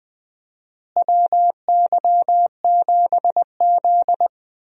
Morse code, heard as a keyed tone, about 21 wpm, 700 Hz. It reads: WY7Z
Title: USA Call Signs - 4 Characters - Repeated 2x - 20wpm